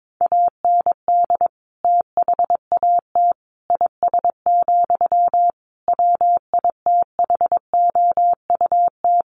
Morse code, heard as a keyed tone, about 22 wpm, 700 Hz. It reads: AND THAT IS, WITHOUT
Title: Sentences from Top 1500 Words 22wpm